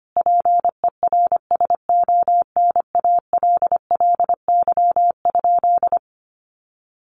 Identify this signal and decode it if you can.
Morse code, heard as a keyed tone, about 25 wpm, 700 Hz. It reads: PERSONALLY?
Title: Sentences from Top 1400 Words 25wpm